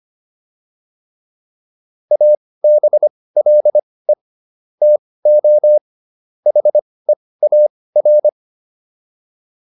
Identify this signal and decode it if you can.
Morse code, heard as a keyed tone, about 25 wpm, spaced as for 18 wpm, 600 Hz. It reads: ABLE TO HEAR